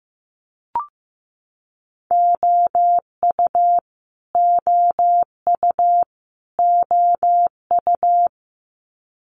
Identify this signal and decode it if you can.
Morse code, heard as a keyed tone, about 15 wpm, 700 Hz. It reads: OU OU OU
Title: Multi-letter Phonemes Repeated 3x 15wpm